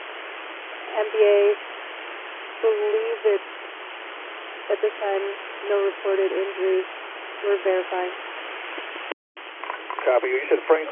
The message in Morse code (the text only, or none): none